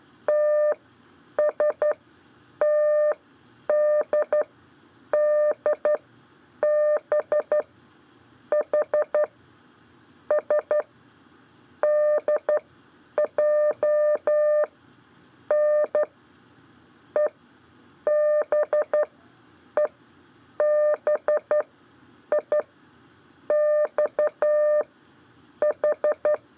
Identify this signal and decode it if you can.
Morse code, heard as a keyed tone, about 12 wpm, spaced as for 8 wpm, 600 Hz. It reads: TSTDDB H S DJN EBEBI XH
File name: 4 LEZIONE CW 1